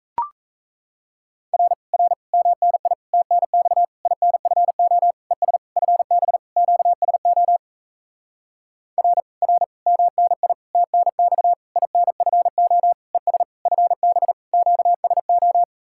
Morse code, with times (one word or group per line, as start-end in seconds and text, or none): 1.5-1.7 R
1.9-2.1 R
2.3-2.9 MNI
3.1-3.8 TNX
4.1-5.1 INFO
5.3-5.6 ES
5.8-6.4 FB
6.6-7.6 QSO
9.0-9.2 R
9.4-9.6 R
9.9-10.5 MNI
10.7-11.5 TNX
11.8-12.9 INFO
13.1-13.4 ES
13.7-14.3 FB
14.5-15.6 QSO